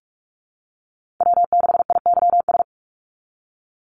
Morse code, heard as a keyed tone, about 46 wpm, 690 Hz. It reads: W6IYS
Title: NA SOTA Chaser Callsigns 45wpm